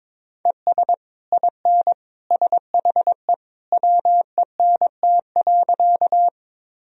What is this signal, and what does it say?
Morse code, heard as a keyed tone, about 22 wpm, 700 Hz. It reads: ES IN SHE WENT.
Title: Sentences from Top 200 Words and Abbreviations 22wpm